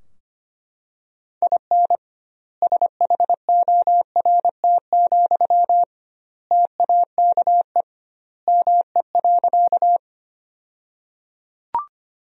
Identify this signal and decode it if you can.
Morse code, heard as a keyed tone, about 25 wpm, 700 Hz. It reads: IN SHORT, TAKE ME.
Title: 2x Word-Spacing - Sentences from Top 400 Words - 25wpm